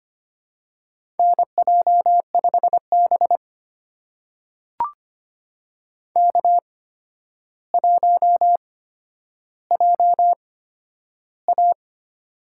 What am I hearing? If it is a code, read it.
Morse code, heard as a keyed tone, about 25 wpm, spaced as for 11 wpm, 700 Hz. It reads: NJ5B K1JA
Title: USA Call Signs - 4 Characters - 8x Character Spacing - 25wpm